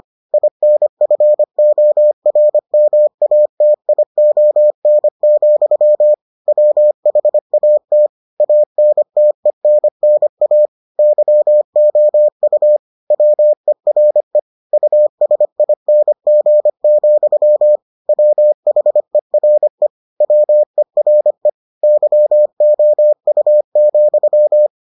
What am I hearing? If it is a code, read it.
Morse code, heard as a keyed tone, about 25 wpm, 600 Hz. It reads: INFORMATION, WHAT ANTENNA YOU WERE USING, WHERE WERE YOU,